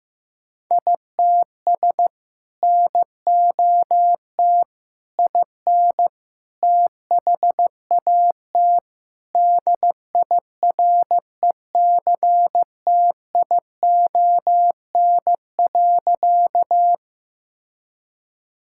Morse code, heard as a keyed tone, about 16 wpm, 700 Hz. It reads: ITS NOT IN THAT DIRECTION.
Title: Sentences from Top 1000 Words - Review Part 5 - 15wpm